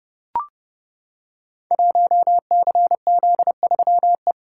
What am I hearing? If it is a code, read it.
Morse code, heard as a keyed tone, about 30 wpm, 700 Hz. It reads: 1CZ3E
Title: Code Groups - 5 Letter:Numbers - 30wpm